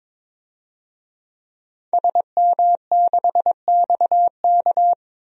Morse code, heard as a keyed tone, about 22 wpm, 700 Hz. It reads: SM6XK
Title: DXCC-100 Call Signs - 5 Characters - 22wpm